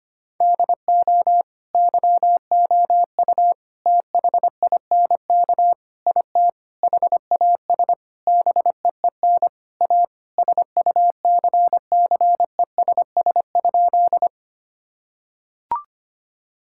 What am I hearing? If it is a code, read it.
Morse code, heard as a keyed tone, about 25 wpm, 700 Hz. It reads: DO YOU THINK IT HAS BEEN A SUCCESS?
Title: Sentences from Top 1000 Words - Review Part 6 - 25wpm